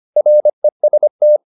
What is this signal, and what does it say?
Morse code, heard as a keyed tone, about 25 wpm, 600 Hz. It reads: REST